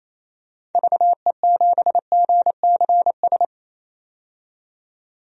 Morse code, heard as a keyed tone, about 28 wpm, 700 Hz. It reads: VE7GCS
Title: DXCC-100 Call Signs - 6 Characters - 28wpm